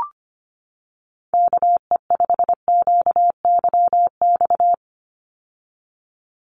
Morse code, heard as a keyed tone, about 25 wpm, 700 Hz. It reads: KE5QYX
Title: USA Call Signs 25wpm